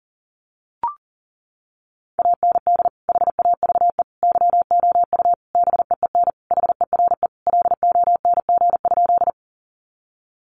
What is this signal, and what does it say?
Morse code, heard as a keyed tone, about 40 wpm, 700 Hz. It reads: AND HAVE YOU BEEN HERE LONG?
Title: Sentences from Top 200 Words 40wpm